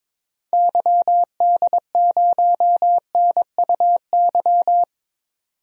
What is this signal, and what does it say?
Morse code, heard as a keyed tone, about 22 wpm, 700 Hz. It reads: YD0NUY
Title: DXCC-100 Call Signs - 6 Characters - 22wpm